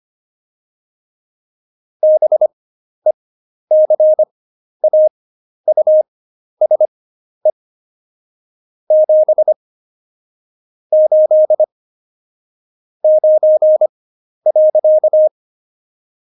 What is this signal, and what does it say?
Morse code, heard as a keyed tone, about 25 wpm, spaced as for 11 wpm, 620 Hz. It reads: BECAUSE 7 8 9.
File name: Jokes-25-char-10-eff